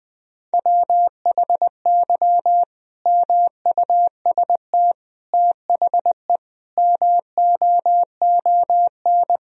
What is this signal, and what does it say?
Morse code, heard as a keyed tone, about 20 wpm, 700 Hz. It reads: WHY MUST THE MOON
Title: Sentences from Top 1000 Words - Review Part 3 - 20wpm